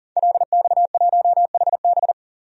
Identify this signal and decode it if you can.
Morse code, heard as a keyed tone, about 40 wpm, 700 Hz. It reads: LX1HB
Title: DXCC-100 Call Signs - 5 Characters - 40wpm